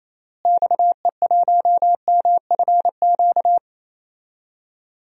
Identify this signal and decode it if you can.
Morse code, heard as a keyed tone, about 28 wpm, 700 Hz. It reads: XE1MFQ
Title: DXCC-100 Call Signs - 6 Characters - Repeated 2x - 28wpm